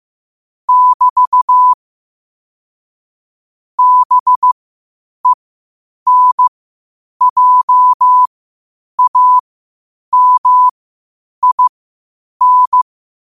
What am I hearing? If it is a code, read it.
Morse code, heard as a keyed tone, about 15 wpm, spaced as for 8 wpm, 1000 Hz. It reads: <BT> BENJAMIN
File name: qotdfiles / 8wpm